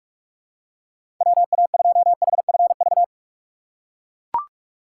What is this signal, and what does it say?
Morse code, heard as a keyed tone, about 44 wpm, 700 Hz. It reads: WA2HFV
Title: USA Call Signs - 6 Characters - 6x Character Spacing - 45wpm